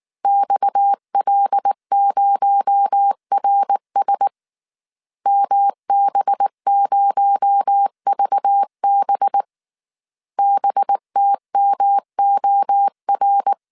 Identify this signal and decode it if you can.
Morse code, heard as a keyed tone, about 19 wpm, 800 Hz. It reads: XL0RS MB0VB BTMOR
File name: CW / DET / 1 6 1500C S A 18wpm 2255